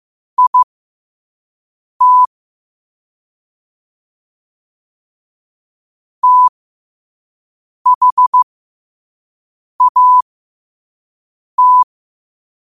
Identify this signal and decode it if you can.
Morse code, heard as a keyed tone, about 15 wpm, spaced as for 5 wpm, 1000 Hz. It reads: IT THAT
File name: qotdfiles / 5wpm